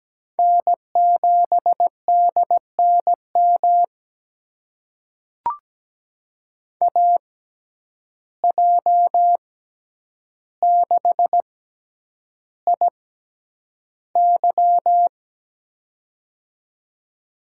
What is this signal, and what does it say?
Morse code, heard as a keyed tone, about 17 wpm, spaced as for 9 wpm, 700 Hz. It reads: N7DNM AJ6IY
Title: USA Call Signs - 5 Characters - 6x Character Spacing - 17wpm